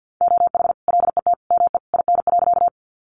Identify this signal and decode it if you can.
Morse code, heard as a keyed tone, about 50 wpm, 700 Hz. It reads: OH LET ME IN.
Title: Sentences from Top 500 Words 50wpm